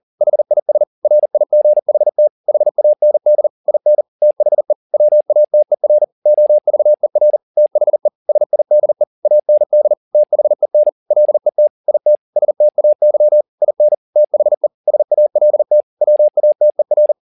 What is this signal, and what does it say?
Morse code, heard as a keyed tone, about 40 wpm, 600 Hz. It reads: HIS RIGHT HAND IN THE WATER OVER THE SIDE AND THEN LET IT STAY IN THE SALT WATER
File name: the old man and the sea 2 40wpm